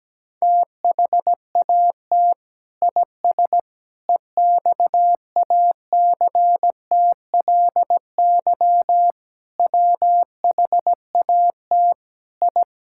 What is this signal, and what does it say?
Morse code, heard as a keyed tone, about 17 wpm, 700 Hz. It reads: THAT IS EXACTLY WHAT I
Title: Sentences from Top 1000 Words - Review Part 6 - 17wpm